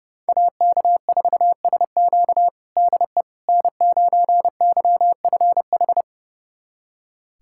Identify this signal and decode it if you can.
Morse code, heard as a keyed tone, about 30 wpm, 700 Hz. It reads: AK4SQ DE N9YFH
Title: USA Callsign DE Callsign - 5 Characters - 30wpm